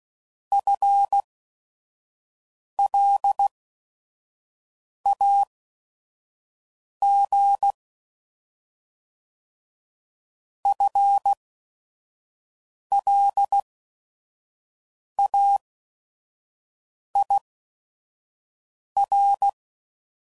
Morse code, heard as a keyed tone, about 16 wpm, spaced as for 5 wpm, 800 Hz. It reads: FLAG FLAIR